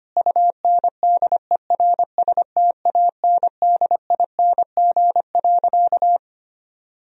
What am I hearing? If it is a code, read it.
Morse code, heard as a keyed tone, about 25 wpm, 700 Hz. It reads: UNDERSTANDING.
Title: Sentences from Top 1000 Words - Review Part 3 - 25wpm